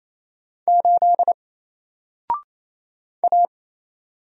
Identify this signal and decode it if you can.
Morse code, heard as a keyed tone, about 27 wpm, 700 Hz. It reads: 8 A
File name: Single-Letter-Number-28wpm